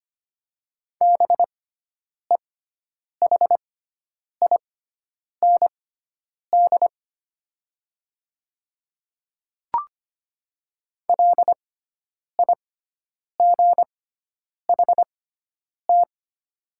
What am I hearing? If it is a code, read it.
Morse code, heard as a keyed tone, about 25 wpm, spaced as for 9 wpm, 690 Hz. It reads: BEHIND LIGHT
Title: 6x Character Spacing - Top 500 Words - 25wpm